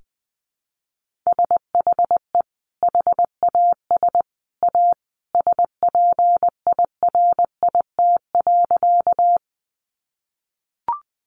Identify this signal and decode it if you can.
Morse code, heard as a keyed tone, about 20 wpm, 700 Hz. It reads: SHE HAS A SPIRIT.
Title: Sentences from Top 1400 Words 20wpm